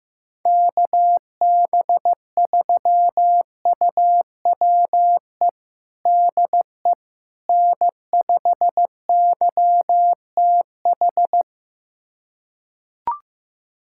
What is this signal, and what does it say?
Morse code, heard as a keyed tone, about 15 wpm, 700 Hz. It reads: KB3UWE DE N5YTH
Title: USA Callsign DE Callsign 15wpm